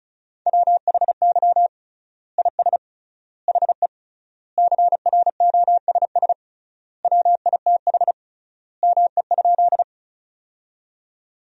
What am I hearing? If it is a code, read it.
Morse code, heard as a keyed tone, about 34 wpm, 700 Hz. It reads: WHY IS HE CROSS WITH ME?
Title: 3x Word-Spacing - Sentences from Top 300 Words - 35wpm